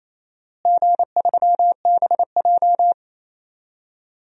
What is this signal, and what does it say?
Morse code, heard as a keyed tone, about 28 wpm, 700 Hz. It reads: G3BJ
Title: DXCC-100 Call Signs - 4 Characters - Repeated 2x - 28wpm